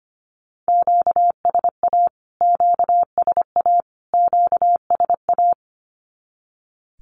Morse code, heard as a keyed tone, about 25 wpm, 700 Hz. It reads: QSA QSA QSA
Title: Q-Codes Repeated 3x 25wpm